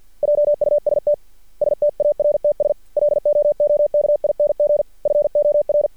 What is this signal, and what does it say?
Morse code, heard as a keyed tone, about 50 wpm, 590 Hz. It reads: JUST STARTS LOOKING FOR
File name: archived / CRSnet